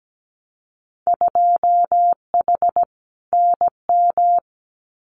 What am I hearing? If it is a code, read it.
Morse code, heard as a keyed tone, about 17 wpm, 700 Hz. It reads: 2H NM
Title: Winter Field Day Exchanges Encore 17wpm